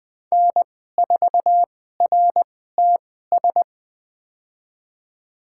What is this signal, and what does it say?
Morse code, heard as a keyed tone, about 20 wpm, spaced as for 14 wpm, 700 Hz. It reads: N4RTS
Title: USA Call Signs - 5 Characters - 2x Character Spacing - 20wpm